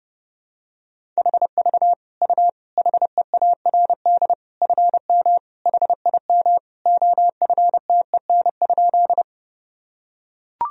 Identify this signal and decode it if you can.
Morse code, heard as a keyed tone, about 30 wpm, 700 Hz. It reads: HV U HEARD FM HIM OFTEN?
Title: Sentences from Top 400 Words and Abbreviations 30wpm